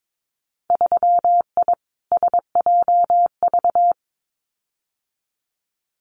Morse code, heard as a keyed tone, about 22 wpm, 700 Hz. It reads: 3I SJV
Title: Winter Field Day Exchanges Encore 22wpm